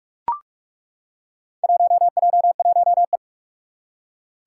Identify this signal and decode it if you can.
Morse code, heard as a keyed tone, about 42 wpm, 700 Hz. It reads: 1J1E
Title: Four Challenging Letter-Number Combinations 45wpm